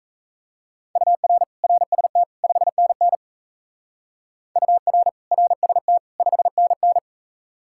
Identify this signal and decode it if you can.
Morse code, heard as a keyed tone, about 34 wpm, 700 Hz. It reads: UR RST 5NN UR RST 5NN
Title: Speed-Racing Common QSO Phrases Encore - 28wpm